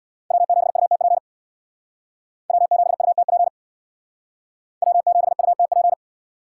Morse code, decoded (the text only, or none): W7UTP W7UTP W7UTP